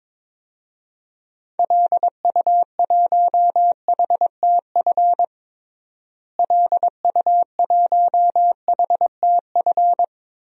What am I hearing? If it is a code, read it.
Morse code, heard as a keyed tone, about 21 wpm, 700 Hz. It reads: LU1HTF LU1HTF
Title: DXCC-100 Call Signs - 6 Characters - Repeated 2x - 22wpm